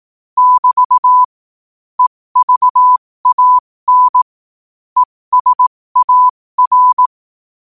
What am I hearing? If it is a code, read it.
Morse code, heard as a keyed tone, about 18 wpm, spaced as for 15 wpm, 1000 Hz. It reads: <BT> EVAN ESAR
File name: qotdfiles / 15wpm